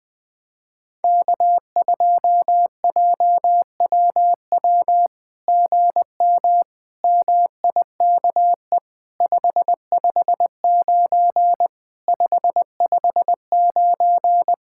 Morse code, heard as a keyed tone, about 20 wpm, 700 Hz. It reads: K2JWW GM MIKE 559 559
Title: SOTA QSO Practice 20wpm